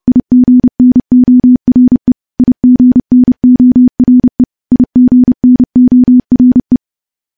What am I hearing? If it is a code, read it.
Morse code, heard as a keyed tone, about 31 wpm, 250 Hz. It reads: IGNORE IGNORE IGNORE
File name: morse / morse-training-30WPM